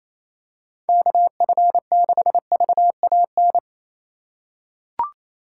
Morse code, heard as a keyed tone, about 28 wpm, 700 Hz. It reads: KF6VAN
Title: USA Call Signs - 6 Characters - 3x Character Spacing - 28wpm